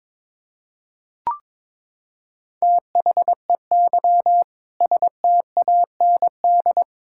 Morse code, heard as a keyed tone, about 22 wpm, 700 Hz. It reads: THEY STAND
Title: Sentences from Top 1000 Words - Review Part 5 - 22wpm